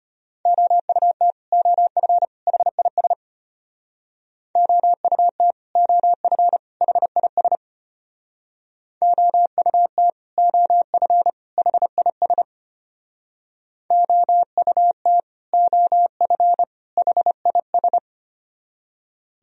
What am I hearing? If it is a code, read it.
Morse code, heard as a keyed tone, about 33 wpm, 700 Hz. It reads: OUT OF HIS OUT OF HIS OUT OF HIS OUT OF HIS